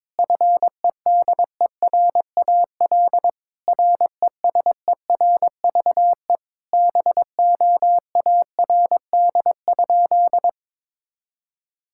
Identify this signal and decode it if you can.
Morse code, heard as a keyed tone, about 22 wpm, 700 Hz. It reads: FEDERAL RESERVE BOARD?
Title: Sentences from Top 1000 Words - Review Part 6 - 22wpm